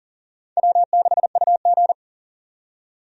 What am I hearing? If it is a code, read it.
Morse code, heard as a keyed tone, about 40 wpm, 700 Hz. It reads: W6UG
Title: USA Call Signs - 4 Characters - 40wpm